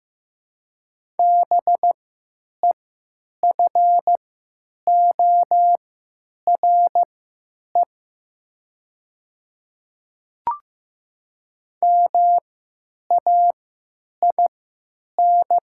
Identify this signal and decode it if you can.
Morse code, heard as a keyed tone, about 15 wpm, spaced as for 9 wpm, 700 Hz. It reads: BEFORE MAIN
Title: 3x Character Spacing - Top 300 Words - 15wpm